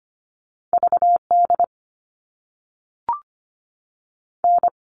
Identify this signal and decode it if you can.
Morse code, heard as a keyed tone, about 25 wpm, 700 Hz. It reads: V D N
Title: All Cut Numbers - 2-Digit 25wpm